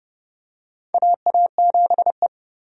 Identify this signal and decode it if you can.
Morse code, heard as a keyed tone, about 30 wpm, 700 Hz. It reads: AA7E
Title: USA Call Signs - 4 Characters - 30wpm